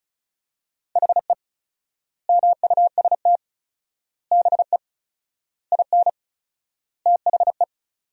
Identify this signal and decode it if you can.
Morse code, heard as a keyed tone, about 36 wpm, 700 Hz. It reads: HE MUST BE IN THE
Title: 4x Word-Spacing - Sentences from Top 500 Words - 35wpm